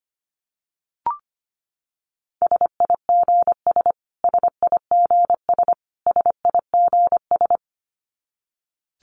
Morse code, heard as a keyed tone, about 25 wpm, 700 Hz. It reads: SIGS SIGS SIGS
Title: Abbreviations Repeated 3x - 25wpm